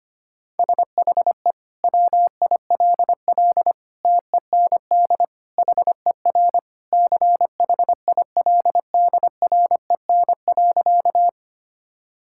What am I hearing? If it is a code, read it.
Morse code, heard as a keyed tone, about 25 wpm, 700 Hz. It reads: SHE WILL TEND HER CHILDREN.
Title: Sentences from Top 1000 Words - Review Part 5 - 25wpm